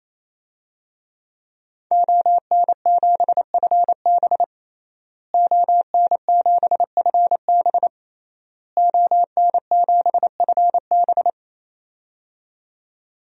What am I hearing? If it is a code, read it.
Morse code, heard as a keyed tone, about 28 wpm, 700 Hz. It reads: ON7FB ON7FB ON7FB
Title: DXCC-100 Call Signs - 5 Characters - Repeated 3x - 28wpm